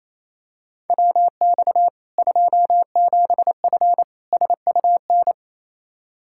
Morse code, heard as a keyed tone, about 28 wpm, 700 Hz. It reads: WX 27F SUN
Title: Practice QSO - Exchange 2 - Info - 28wpm